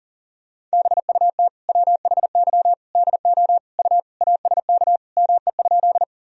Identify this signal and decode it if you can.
Morse code, heard as a keyed tone, about 40 wpm, 700 Hz. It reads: BUT WHY DO U ASK ME?
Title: Sentences from Top 200 Words and Abbreviations 40wpm